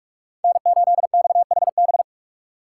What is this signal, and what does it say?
Morse code, heard as a keyed tone, about 43 wpm, 700 Hz. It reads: N8XHB